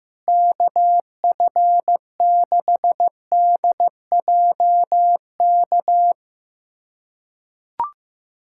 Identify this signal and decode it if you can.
Morse code, heard as a keyed tone, about 16 wpm, 700 Hz. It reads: KF6DJK
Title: USA Callsign DE Callsign - 6 Characters - 15wpm